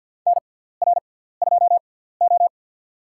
Morse code, heard as a keyed tone, about 49 wpm, spaced as for 17 wpm, 700 Hz. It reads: NR2O
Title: USA Call Signs - 4 Characters - 6x Character Spacing - 50wpm